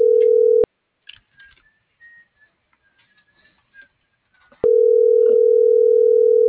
Ringback tone (North America), 440 + 480 Hz.